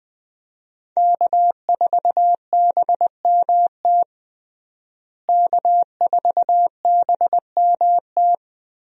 Morse code, heard as a keyed tone, about 20 wpm, 700 Hz. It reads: K4BMT K4BMT
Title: USA Call Signs - 5 Characters - Repeated 3x - 20wpm